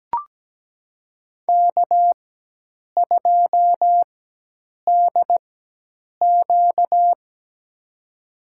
Morse code, heard as a keyed tone, about 17 wpm, spaced as for 8 wpm, 700 Hz. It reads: K2DQ
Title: USA Call Signs - 4x Character Spacing - 17wpm